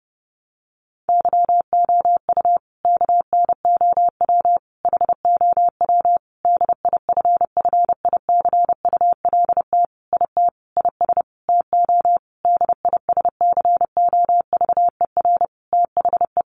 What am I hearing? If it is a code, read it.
Morse code, heard as a keyed tone, about 30 wpm, 700 Hz. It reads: YOU KNOW HOW DIFFICULT IT IS TO DISCOVER THE
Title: Sentences from Top 1100 Words 30wpm